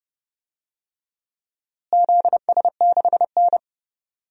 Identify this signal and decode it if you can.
Morse code, heard as a keyed tone, about 30 wpm, 700 Hz. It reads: ZS6N
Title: DXCC-100 Call Signs - 4 Characters - Repeated 3x - 30wpm